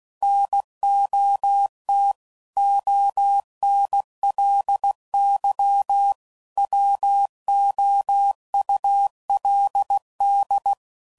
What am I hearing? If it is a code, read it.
Morse code, heard as a keyed tone, about 16 wpm, 800 Hz. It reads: NOT ONLY WOULD
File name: CW Code Groups @15WPM / Technical Subjects / AND Gate